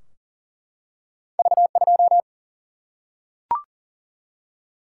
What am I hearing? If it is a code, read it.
Morse code, heard as a keyed tone, about 40 wpm, 700 Hz. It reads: V2
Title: Single-Letter Single-Number 40wpm